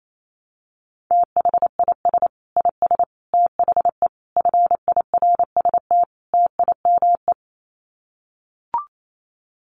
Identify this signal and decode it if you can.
Morse code, heard as a keyed tone, about 28 wpm, 700 Hz. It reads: THIS IS THE FIRST TIME
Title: Sets of 5 Words 28wpm